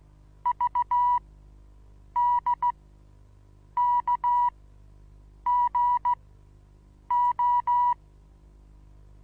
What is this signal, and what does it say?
Morse code, heard as a keyed tone, about 16 wpm, spaced as for 7 wpm, 970 Hz. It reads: VDKGO
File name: Onlinelehrgang / morsen / DJ4UF Morsen25